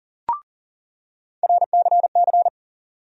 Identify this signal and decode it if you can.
Morse code, heard as a keyed tone, about 40 wpm, 700 Hz. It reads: RCC
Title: USA Call Sign Postfixes - 3 Letters 40wpm